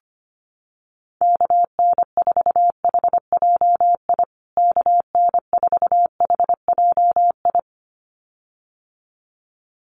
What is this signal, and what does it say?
Morse code, heard as a keyed tone, about 25 wpm, 700 Hz. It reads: KN4HJI KN4HJI
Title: USA Call Signs Repeated 2x 25wpm